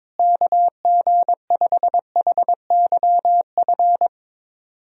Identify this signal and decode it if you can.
Morse code, heard as a keyed tone, about 22 wpm, 700 Hz. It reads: KG5HYF
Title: USA Call Signs - 6 Characters - Repeated 3x - 22wpm